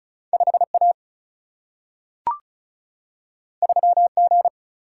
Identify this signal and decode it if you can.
Morse code, heard as a keyed tone, about 36 wpm, 700 Hz. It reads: 5 A 3 G
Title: Single-Letter Single-Number 35wpm